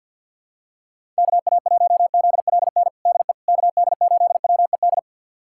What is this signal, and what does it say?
Morse code, heard as a keyed tone, about 52 wpm, 700 Hz. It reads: KA1ZLN DE KD8WED
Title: USA Callsign DE Callsign - 6 Characters - 50wpm